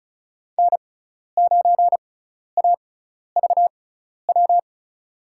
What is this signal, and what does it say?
Morse code, heard as a keyed tone, about 36 wpm, spaced as for 12 wpm, 700 Hz. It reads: N9AVW
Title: USA Call Signs - 6x Character Spacing - 35wpm